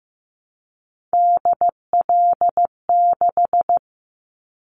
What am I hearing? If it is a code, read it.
Morse code, heard as a keyed tone, about 15 wpm, 700 Hz. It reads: DL6
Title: Callsign Prefixes for Top 25 DXCC - 15wpm